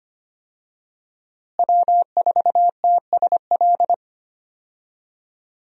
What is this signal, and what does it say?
Morse code, heard as a keyed tone, about 25 wpm, 700 Hz. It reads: W4TSL
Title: USA Call Signs - 5 Characters - Repeated 2x - 25wpm